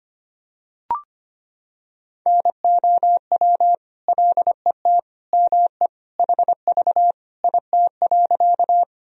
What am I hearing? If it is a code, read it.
Morse code, heard as a keyed tone, about 25 wpm, 700 Hz. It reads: NOW LET ME HV IT.
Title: Sentences from Top 300 Words and Abbreviations 25wpm